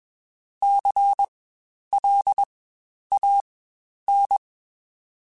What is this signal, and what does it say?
Morse code, heard as a keyed tone, about 20 wpm, spaced as for 10 wpm, 800 Hz. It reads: CLAN
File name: Spaced Alphas / Spaced Alphas mp3s / Cs sp @20